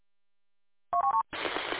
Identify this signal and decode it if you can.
There is no code.